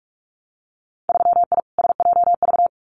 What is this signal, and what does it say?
Morse code, heard as a keyed tone, about 46 wpm, 700 Hz. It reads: 3I SJV
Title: Winter Field Day Exchanges Encore 45wpm